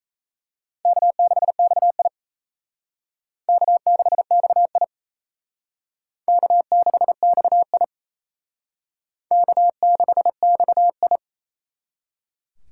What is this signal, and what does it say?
Morse code, heard as a keyed tone, about 38 wpm, 700 Hz. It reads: K6XI K6XI K6XI K6XI